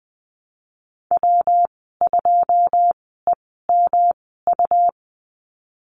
Morse code, heard as a keyed tone, about 20 wpm, spaced as for 14 wpm, 700 Hz. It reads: W2EMU
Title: 140 Lesson 23g - Letter f - Call Signs 20wpm